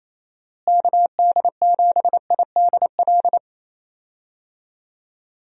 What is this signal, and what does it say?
Morse code, heard as a keyed tone, about 28 wpm, 700 Hz. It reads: KD7IDL
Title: USA Call Signs - 6 Characters - Repeated 2x - 28wpm